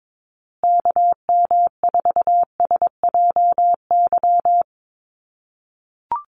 Morse code, heard as a keyed tone, about 22 wpm, 700 Hz. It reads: KM4SJY
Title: USA Call Signs Repeated 2x 22wpm